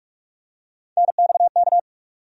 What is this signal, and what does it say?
Morse code, heard as a keyed tone, about 45 wpm, 700 Hz. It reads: NXK
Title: USA Call Sign Postfixes - 3 Letters 45wpm